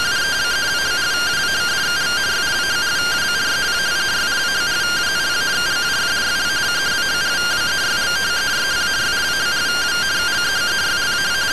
Two-tone teleprinter signal, about 170 Hz shift.
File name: findings / MR CORONAL fr9 -99.gen-01 rtty rx